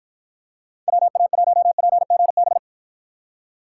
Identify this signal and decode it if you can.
Morse code, heard as a keyed tone, about 52 wpm, 700 Hz. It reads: WA1PGB